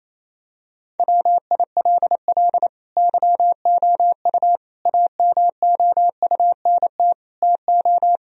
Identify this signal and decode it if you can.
Morse code, heard as a keyed tone, about 28 wpm, 700 Hz. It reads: WILL YOU AMOUNT TO
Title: Sentences from Top 1000 Words - Review Part 4 - 28wpm